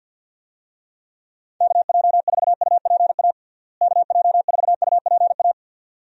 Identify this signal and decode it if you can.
Morse code, heard as a keyed tone, about 53 wpm, 700 Hz. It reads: KJ4UPA KJ4UPA
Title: USA Call Signs - 6 Characters - Repeated 2x - 50wpm